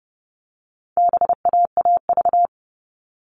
Morse code, heard as a keyed tone, about 30 wpm, 700 Hz. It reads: BAAV
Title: All Cut Numbers - 4-Digit 30wpm